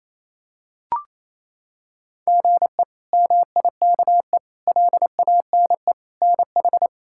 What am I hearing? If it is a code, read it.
Morse code, heard as a keyed tone, about 28 wpm, 700 Hz. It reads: GE MIKE LANE NH
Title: K1USN SST QSO Practice - 28wpm